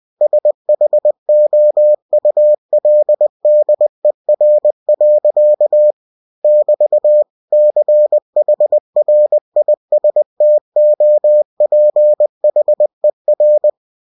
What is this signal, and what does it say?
Morse code, heard as a keyed tone, about 20 wpm, 600 Hz. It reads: SHOULDER. <BT> CHRISTOPHER
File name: cw-books / winnie-the-pooh / winnie-20wpm-page-0098